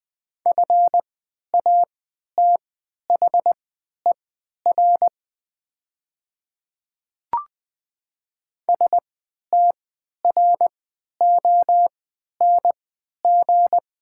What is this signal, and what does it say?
Morse code, heard as a keyed tone, about 20 wpm, spaced as for 11 wpm, 700 Hz. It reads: FATHER STRONG
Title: 3x Character Spacing - Top 500 Words - 20wpm